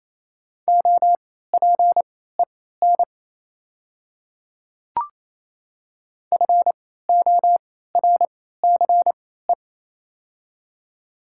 Morse code, heard as a keyed tone, about 28 wpm, spaced as for 16 wpm, 700 Hz. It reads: OPEN FORCE
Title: 3x Character Spacing - Top 500 Words - 28wpm